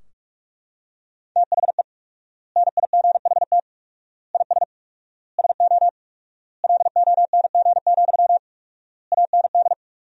Morse code, heard as a keyed tone, about 42 wpm, 700 Hz. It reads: THE NIGHT IS SO LONG, AND
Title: 4x Word-Spacing - Sentences from Top 500 Words - 45wpm